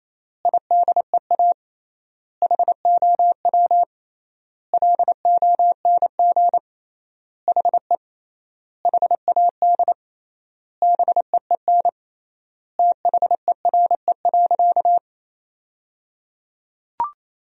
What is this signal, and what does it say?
Morse code, heard as a keyed tone, about 28 wpm, 700 Hz. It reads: IDEA HOW LONG HE HAD BEEN THERE.
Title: 3x Word-Spacing - Sentences from Top 300 Words - 28wpm